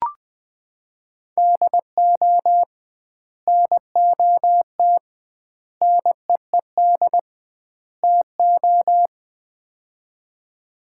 Morse code, heard as a keyed tone, about 20 wpm, 700 Hz. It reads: DO NOT NEED TO
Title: 2x Word-Spacing - Sets of 4 Words - 20wpm